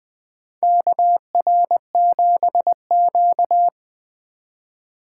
Morse code, heard as a keyed tone, about 20 wpm, 700 Hz. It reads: KR7Q
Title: 2023 SOTA Chaser Call Signs - 20wpm